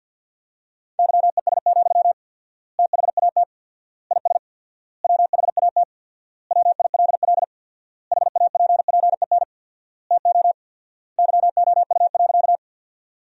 Morse code, heard as a keyed tone, about 52 wpm, 700 Hz. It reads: YES, THAT IS WHAT WILL HAPPEN TO YOU.
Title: 4x Word-Spacing - Sentences from Top 500 Words - 50wpm